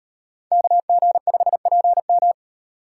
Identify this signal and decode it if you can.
Morse code, heard as a keyed tone, about 37 wpm, 700 Hz. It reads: KG5PM